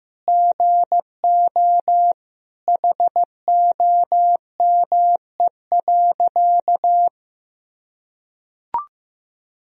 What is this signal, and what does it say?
Morse code, heard as a keyed tone, about 15 wpm, 700 Hz. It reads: GO HOME.
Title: Sentences from Top 1000 Words - Review Part 2 - 15wpm